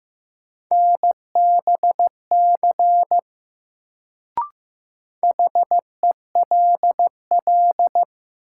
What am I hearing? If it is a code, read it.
Morse code, heard as a keyed tone, about 15 wpm, 700 Hz. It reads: NBC HELL
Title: Sets of 5 Words Encore - New Word Drill - 15wpm